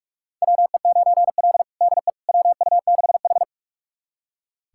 Morse code, heard as a keyed tone, about 44 wpm, 700 Hz. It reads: WE9L DE WU6H
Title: USA Callsign DE Callsign - 4 Characters - 45wpm